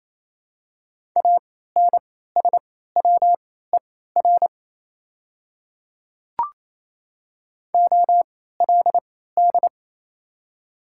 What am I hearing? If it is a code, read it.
Morse code, heard as a keyed tone, about 27 wpm, spaced as for 16 wpm, 690 Hz. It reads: ANSWER OLD
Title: 3x Character Spacing - Top 300 Words - 28wpm